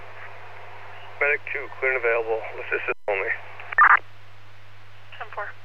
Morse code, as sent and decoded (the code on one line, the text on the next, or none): none